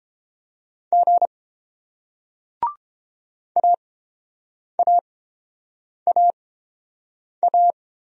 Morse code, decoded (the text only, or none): G AAAA